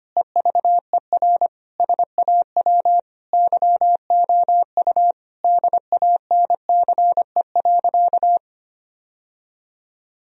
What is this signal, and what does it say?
Morse code, heard as a keyed tone, about 25 wpm, 700 Hz. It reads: EVER SAW YOU DANCE.
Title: Sentences from Top 1000 Words - Review Part 4 - 25wpm